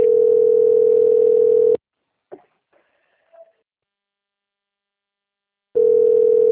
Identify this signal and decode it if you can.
Ringback tone (North America), 440 + 480 Hz.